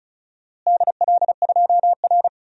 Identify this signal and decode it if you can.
Morse code, heard as a keyed tone, about 34 wpm, 700 Hz. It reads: DL2R
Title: DXCC-100 Call Signs - 4 Characters - Repeated 3x - 35wpm